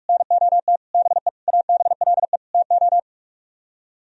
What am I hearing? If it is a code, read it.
Morse code, heard as a keyed tone, about 50 wpm, 700 Hz. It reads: NOT BE ABLE TO